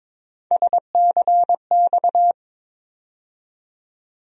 Morse code, heard as a keyed tone, about 22 wpm, 700 Hz. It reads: SCX
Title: USA Call Sign Postfixes - 3 Letters 22wpm